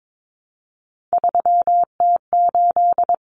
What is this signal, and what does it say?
Morse code, heard as a keyed tone, about 22 wpm, 700 Hz. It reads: 3T8
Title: Common Cut Numbers - 3-Digit 22wpm